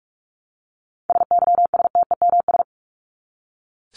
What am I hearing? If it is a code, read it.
Morse code, heard as a keyed tone, about 44 wpm, 700 Hz. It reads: SYSTEMS
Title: ES:S Suffix - Top 100 Words with Base Words 45wpm